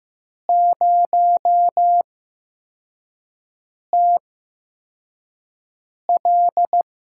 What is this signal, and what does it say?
Morse code, heard as a keyed tone, about 15 wpm, spaced as for 4 wpm, 700 Hz. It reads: 0TL
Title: USA Call Signs - 8x Character Spacing - 15wpm